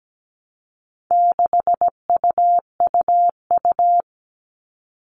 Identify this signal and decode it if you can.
Morse code, heard as a keyed tone, about 17 wpm, 700 Hz. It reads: 6UUU
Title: All Cut Numbers - 4-Digit 17wpm